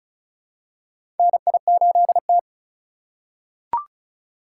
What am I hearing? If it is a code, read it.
Morse code, heard as a keyed tone, about 34 wpm, 700 Hz. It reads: NI8T
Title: USA Call Signs - 4 Characters - 6x Character Spacing - 35wpm